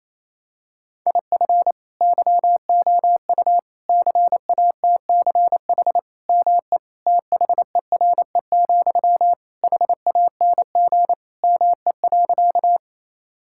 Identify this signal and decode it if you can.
Morse code, heard as a keyed tone, about 28 wpm, 700 Hz. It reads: IF YOU CATCH ME THERE, HANG ME.
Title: Sentences from Top 1600 Words 28wpm